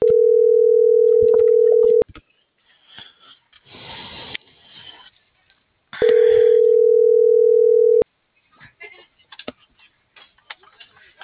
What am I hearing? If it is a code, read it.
Ringback tone (North America), 440 + 480 Hz.
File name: CENTRAL SKYBLUE / out lopez.brandon